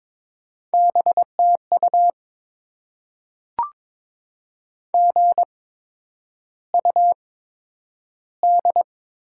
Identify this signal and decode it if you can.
Morse code, heard as a keyed tone, about 22 wpm, spaced as for 9 wpm, 700 Hz. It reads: BTU GUD
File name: Top 20 QSO Elements - 8x Character Spacing - 22wpm